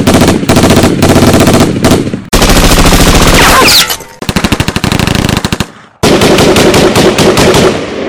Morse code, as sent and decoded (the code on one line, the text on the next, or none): none